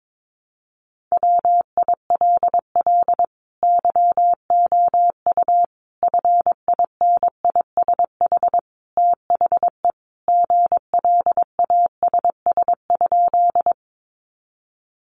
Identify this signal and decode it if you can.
Morse code, heard as a keyed tone, about 22 wpm, 700 Hz. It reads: WILL YOU FINISH THE GLASS?
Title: Sentences from Top 1300 Words 22wpm